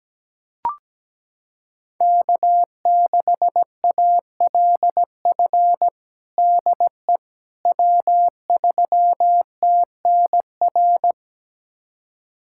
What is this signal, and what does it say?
Morse code, heard as a keyed tone, about 17 wpm, 700 Hz. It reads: K6ALF DE W3TNR
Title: USA Callsign DE Callsign - 5 Characters - 17wpm